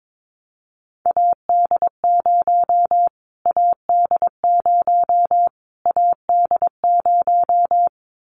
Morse code, heard as a keyed tone, about 22 wpm, 700 Hz. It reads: AD0 AD0 AD0
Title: Callsign Prefixes for Top 25 DXCC - Repeated 3x - 22wpm